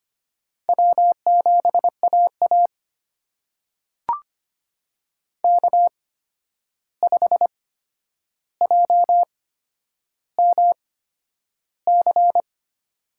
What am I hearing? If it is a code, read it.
Morse code, heard as a keyed tone, about 25 wpm, spaced as for 10 wpm, 700 Hz. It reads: W7AA K5JMC
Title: USA Call Signs - 8x Character Spacing - 25wpm